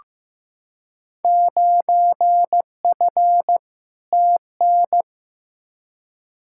Morse code, heard as a keyed tone, about 15 wpm, 700 Hz. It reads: 9F TN
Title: Field Day Exchanges Encore - 15wpm